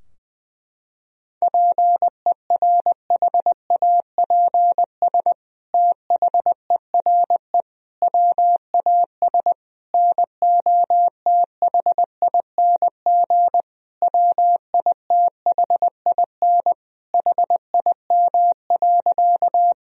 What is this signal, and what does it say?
Morse code, heard as a keyed tone, about 20 wpm, 700 Hz. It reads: PERHAPS THERE WAS NOTHING WITHIN HIM.
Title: Sentences from Top 1000 Words - Review Part 4 - 20wpm